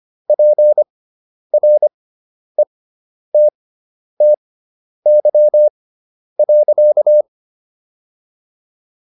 Morse code, heard as a keyed tone, about 25 wpm, spaced as for 10 wpm, 600 Hz. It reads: PRETTY.